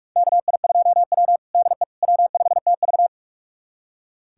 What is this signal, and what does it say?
Morse code, heard as a keyed tone, about 41 wpm, 700 Hz. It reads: KI2W DE W5TV
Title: USA Callsign DE Callsign - 4 Characters - 45wpm